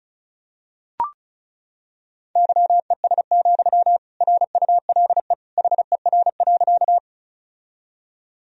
Morse code, heard as a keyed tone, about 34 wpm, 700 Hz. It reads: YES, RULE HER.
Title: Sentences from Top 1000 Words - Review Part 2 - 35wpm